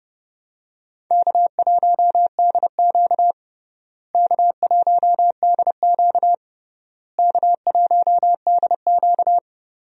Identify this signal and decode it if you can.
Morse code, heard as a keyed tone, about 30 wpm, 700 Hz. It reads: K1DQ K1DQ K1DQ
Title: USA Call Signs - 4 Characters - Repeated 3x - 30wpm